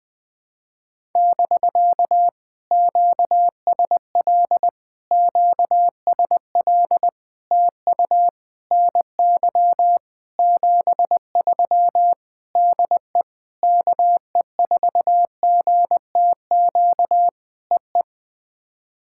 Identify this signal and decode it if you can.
Morse code, heard as a keyed tone, about 20 wpm, 700 Hz. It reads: <BK> QSL QSL TU NY 73 DE KE4GTQ EE
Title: POTA QSO Practice - 20wpm